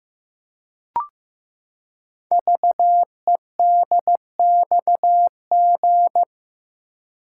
Morse code, heard as a keyed tone, about 15 wpm, 700 Hz. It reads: VEDXG
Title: Code Groups - 5 Letters - 15wpm